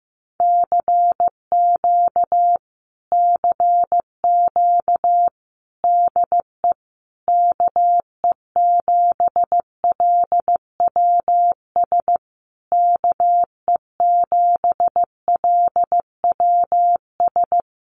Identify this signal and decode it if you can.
Morse code, heard as a keyed tone, about 15 wpm, 700 Hz. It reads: CQ CQ DE KE7LWS KE7LWS
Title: Common QSO Phrases Encore 15wpm